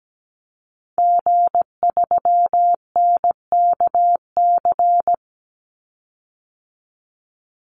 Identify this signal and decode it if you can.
Morse code, heard as a keyed tone, about 17 wpm, 700 Hz. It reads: G3NKC
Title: Call Signs Repeated 3x - 17wpm